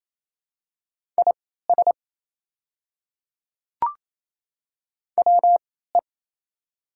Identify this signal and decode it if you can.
Morse code, heard as a keyed tone, about 28 wpm, spaced as for 16 wpm, 700 Hz. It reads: IS WE
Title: 3x Character Spacing - Top 100 Words - 28wpm